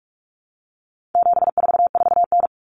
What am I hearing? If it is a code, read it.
Morse code, heard as a keyed tone, about 44 wpm, 700 Hz. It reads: 744N
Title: Common Cut Numbers - 4-Digit 45wpm